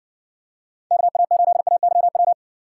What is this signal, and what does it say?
Morse code, heard as a keyed tone, about 53 wpm, 700 Hz. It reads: KA8AYW